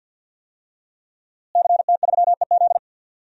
Morse code, heard as a keyed tone, about 51 wpm, 700 Hz. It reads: CT3EZ
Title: DXCC-100 Call Signs - 5 Characters - Repeated 3x - 50wpm